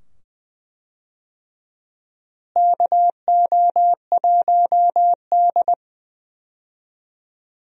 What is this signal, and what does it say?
Morse code, heard as a keyed tone, about 20 wpm, 700 Hz. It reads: KO1D
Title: USA Call Signs - 4 Characters - Repeated 2x - 20wpm